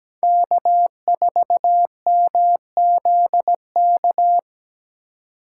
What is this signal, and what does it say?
Morse code, heard as a keyed tone, about 17 wpm, 700 Hz. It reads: K4MZK
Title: USA Call Signs - 5 Characters - 3x Character Spacing - 17wpm